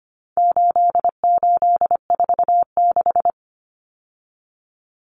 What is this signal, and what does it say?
Morse code, heard as a keyed tone, about 25 wpm, 700 Hz. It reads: 8846
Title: Numbers - 4-Digit 25wpm